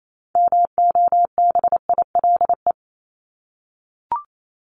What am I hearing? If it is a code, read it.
Morse code, heard as a keyed tone, about 28 wpm, 700 Hz. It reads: MOBILE
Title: Top 1751-2000 Words 28wpm